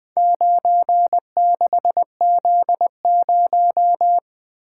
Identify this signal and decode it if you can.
Morse code, heard as a keyed tone, about 21 wpm, 700 Hz. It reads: 96Z0
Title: Four Letter-Number Combinations - 20wpm